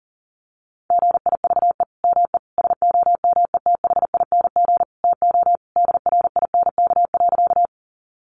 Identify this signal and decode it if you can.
Morse code, heard as a keyed tone, about 40 wpm, 700 Hz. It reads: GIVE ME SOMETHING TO DRINK.
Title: Sentences from Top 1500 Words 40wpm